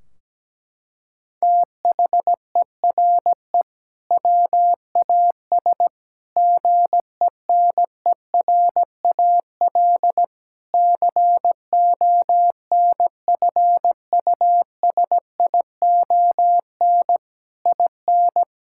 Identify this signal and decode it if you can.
Morse code, heard as a keyed tone, about 17 wpm, 700 Hz. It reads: THERE WAS GENERAL CONFUSION IN
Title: Sentences from Top 1600 Words 17wpm